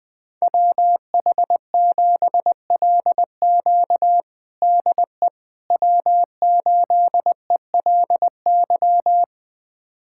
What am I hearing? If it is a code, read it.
Morse code, heard as a keyed tone, about 20 wpm, 700 Hz. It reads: WH7LQ DE W8ELY
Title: USA Callsign DE Callsign - 5 Characters - 20wpm